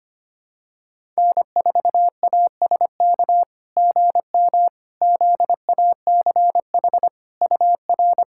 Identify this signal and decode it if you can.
Morse code, heard as a keyed tone, about 25 wpm, 690 Hz. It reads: N4ASK GM ZACH UR
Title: SOTA QSO Practice 25wpm